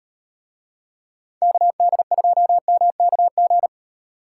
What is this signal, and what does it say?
Morse code, heard as a keyed tone, about 38 wpm, 700 Hz. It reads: KD2MKG